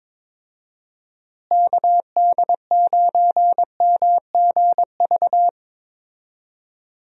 Morse code, heard as a keyed tone, about 22 wpm, 700 Hz. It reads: KD9MGV
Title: USA Call Signs - 6 Characters - Repeated 2x - 22wpm